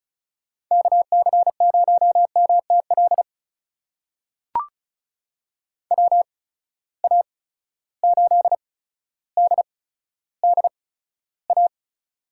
Morse code, heard as a keyed tone, about 36 wpm, spaced as for 15 wpm, 700 Hz. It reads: KC0MTL WA8DDA
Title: USA Call Signs - 6 Characters - 8x Character Spacing - 35wpm